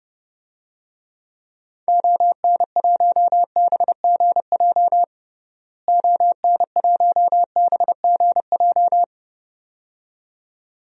Morse code, heard as a keyed tone, about 30 wpm, 700 Hz. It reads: ON1BGJ ON1BGJ
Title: DXCC-100 Call Signs - 6 Characters - Repeated 2x - 30wpm